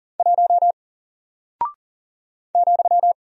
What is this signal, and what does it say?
Morse code, heard as a keyed tone, about 40 wpm, 700 Hz. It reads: 1 ,
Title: Single Character (Letter:Number:Punctuation:Prosign) Alt 40wpm